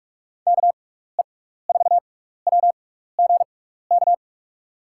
Morse code, heard as a keyed tone, about 46 wpm, spaced as for 16 wpm, 700 Hz. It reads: KE4WGK
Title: USA Call Signs - 6 Characters - 6x Character Spacing - 45wpm